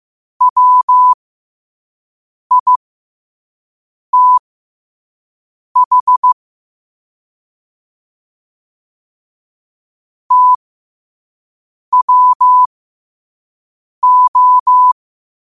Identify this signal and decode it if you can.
Morse code, heard as a keyed tone, about 15 wpm, spaced as for 5 wpm, 1000 Hz. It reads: WITH TWO
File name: qotdfiles / 5wpm